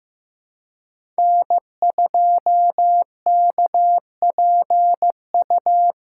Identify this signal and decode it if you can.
Morse code, heard as a keyed tone, about 15 wpm, 700 Hz. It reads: N2KPU
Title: USA Call Signs - 2x Character Spacing - 15wpm